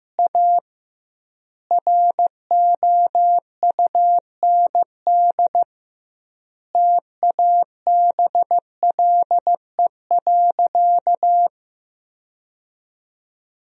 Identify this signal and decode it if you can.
Morse code, heard as a keyed tone, about 15 wpm, 700 Hz. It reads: A ROUND TABLE.
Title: 2x Word-Spacing - Sentences from Top 400 Words - 15wpm